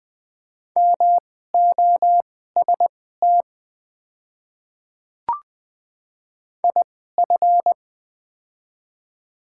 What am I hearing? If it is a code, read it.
Morse code, heard as a keyed tone, about 20 wpm, spaced as for 14 wpm, 700 Hz. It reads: MOST IF
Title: 2x Character Spacing - Top 300 Words - 20wpm